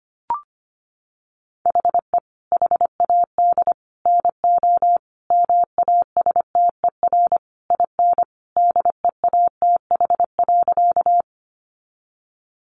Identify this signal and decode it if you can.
Morse code, heard as a keyed tone, about 25 wpm, 700 Hz. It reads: HE HAD NO MASTER IN DEATH.
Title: Sentences from Top 1300 Words 25wpm